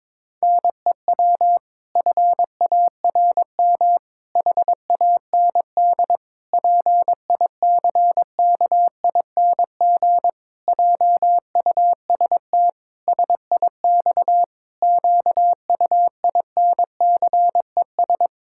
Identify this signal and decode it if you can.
Morse code, heard as a keyed tone, about 22 wpm, 700 Hz. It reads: NEW FARM HAND PICKING JUST SIX QUINCES
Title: Pangram Challenge v2 - 22wpm